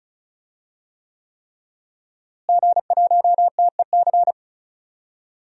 Morse code, heard as a keyed tone, about 36 wpm, 700 Hz. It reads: G1TEC
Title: DXCC-100 Call Signs - 5 Characters - Repeated 3x - 35wpm